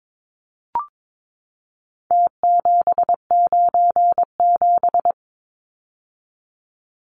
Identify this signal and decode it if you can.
Morse code, heard as a keyed tone, about 22 wpm, 700 Hz. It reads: T797
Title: Numbers - Contest Style - 4-Digit 22wpm